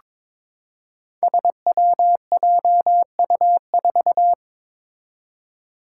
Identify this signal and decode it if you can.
Morse code, heard as a keyed tone, about 23 wpm, 700 Hz. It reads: SWJU4
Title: Five Challenging Letter-Number Combinations 22wpm